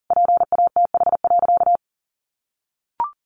Morse code, heard as a keyed tone, about 40 wpm, 700 Hz. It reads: PATH.
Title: A Sound Of Thunder 40wpm